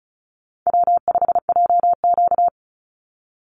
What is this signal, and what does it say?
Morse code, heard as a keyed tone, about 34 wpm, 700 Hz. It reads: W5JQ
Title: Easy Call Signs Repeated 2x - 35wpm